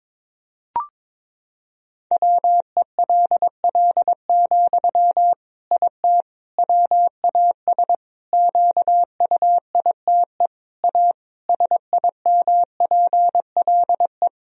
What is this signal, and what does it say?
Morse code, heard as a keyed tone, about 22 wpm, 700 Hz. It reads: WELL, IT WAS QUITE A SIMPLE
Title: Sentences from Top 1000 Words - Review Part 3 - 22wpm